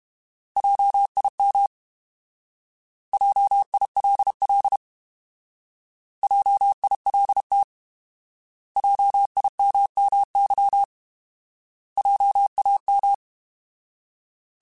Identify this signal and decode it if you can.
Morse code, heard as a keyed tone, about 32 wpm, 800 Hz. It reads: JIM JILL JILT JIMMY JAM
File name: Alphabetical / Alphabetical mp3s / Js @30